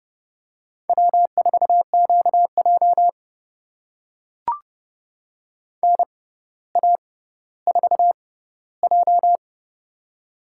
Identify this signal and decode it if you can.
Morse code, heard as a keyed tone, about 30 wpm, spaced as for 15 wpm, 700 Hz. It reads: W4QJ NA4J
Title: USA Call Signs - 4 Characters - 6x Character Spacing - 30wpm